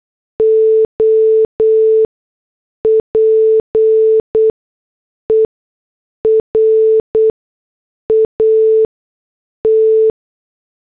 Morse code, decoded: OPERAT